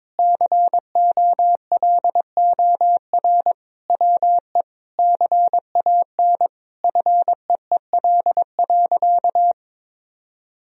Morse code, heard as a keyed tone, about 22 wpm, 700 Hz. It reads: COLOR WE CAN FEEL.
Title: Sentences from Top 1000 Words - Review Part 3 - 22wpm